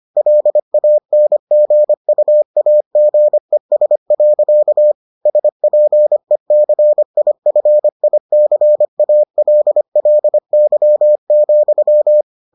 Morse code, heard as a keyed tone, about 25 wpm, 600 Hz. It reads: LANGUAGES. SPECIFICALLY,